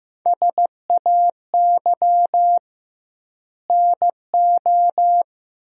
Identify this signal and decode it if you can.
Morse code, heard as a keyed tone, about 16 wpm, 700 Hz. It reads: SAY NO
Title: 2x Word-Spacing - Sentences from Top 200 Words - 15wpm